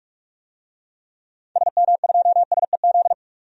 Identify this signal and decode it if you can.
Morse code, heard as a keyed tone, about 43 wpm, 700 Hz. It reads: SM2SEZ
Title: DXCC-100 Call Signs - 6 Characters - 45wpm